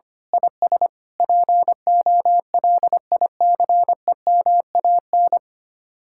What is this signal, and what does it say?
Morse code, heard as a keyed tone, about 24 wpm, 700 Hz. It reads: IS POLICEMAN
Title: Speed-Racing Common QSO Phrases - 25wpm